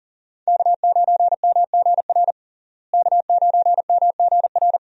Morse code, heard as a keyed tone, about 37 wpm, 700 Hz. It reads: K9MGR K9MGR
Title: USA Call Signs - 5 Characters - Repeated 3x - 40wpm